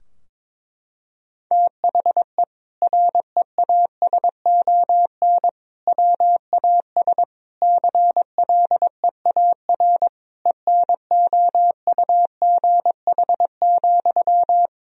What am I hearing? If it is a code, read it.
Morse code, heard as a keyed tone, about 22 wpm, 700 Hz. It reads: THE REASON WAS CLEAR ENOUGH,
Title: Sentences from Top 1000 Words - Review Part 2 - 22wpm